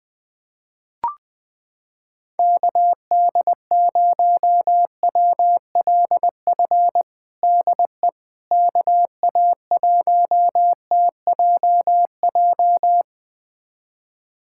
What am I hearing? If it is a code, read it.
Morse code, heard as a keyed tone, about 20 wpm, 700 Hz. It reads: KD0WLF DE KA1TJJ
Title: USA Callsign DE Callsign - 6 Characters - 20wpm